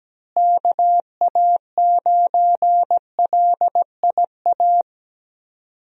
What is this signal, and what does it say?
Morse code, heard as a keyed tone, about 17 wpm, 700 Hz. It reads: KA9LIA
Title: USA Call Signs - 6 Characters - Repeated 2x - 17wpm